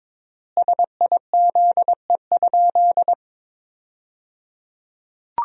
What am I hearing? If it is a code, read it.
Morse code, heard as a keyed tone, about 22 wpm, 700 Hz. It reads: SIZE?
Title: 3x Word-Spacing - Sentences from Top 500 Words - 22wpm